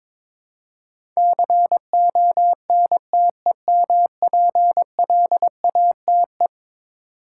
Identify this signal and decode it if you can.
Morse code, heard as a keyed tone, about 22 wpm, 700 Hz. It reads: CONTEMPLATE
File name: practice / 22wpm / Eleven Letter Words - 22wpm